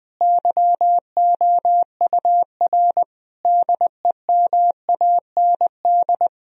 Morse code, heard as a keyed tone, about 20 wpm, 700 Hz. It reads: YOUR DEMAND
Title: Sentences from Top 1000 Words - Review Part 4 - 20wpm